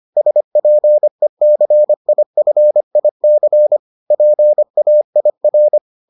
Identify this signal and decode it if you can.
Morse code, heard as a keyed tone, about 25 wpm, 600 Hz. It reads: SPECIFIC PAIR